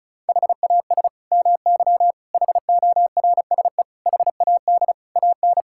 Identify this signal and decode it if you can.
Morse code, heard as a keyed tone, about 34 wpm, 700 Hz. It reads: HAS MY HORSE HAD AN
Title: Sentences from Top 1600 Words 35wpm